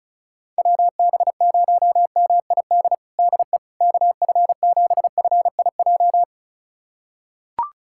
Morse code, heard as a keyed tone, about 34 wpm, 700 Hz. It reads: WB0MID DE KF7FIJ
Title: USA Callsign DE Callsign - 6 Characters - 35wpm